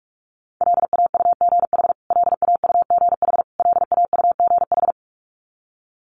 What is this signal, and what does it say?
Morse code, heard as a keyed tone, about 44 wpm, 700 Hz. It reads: LAUGH LAUGH LAUGH
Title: Top 500 Words Repeated 45wpm